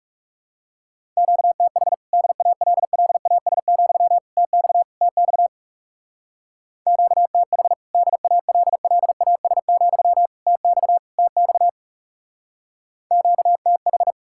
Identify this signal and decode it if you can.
Morse code, heard as a keyed tone, about 37 wpm, 700 Hz. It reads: QTH DALLAS, TX TX QTH DALLAS, TX TX QTH
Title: Speed-Racing Common QSO Phrases Encore - 30wpm